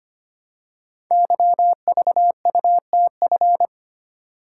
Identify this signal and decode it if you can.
Morse code, heard as a keyed tone, about 25 wpm, 700 Hz. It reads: YVUTF
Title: Code Groups - 5 Letter:Numbers - 25wpm